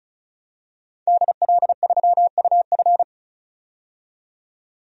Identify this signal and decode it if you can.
Morse code, heard as a keyed tone, about 36 wpm, 700 Hz. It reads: DL3UF
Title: DXCC-100 Call Signs - 5 Characters - 35wpm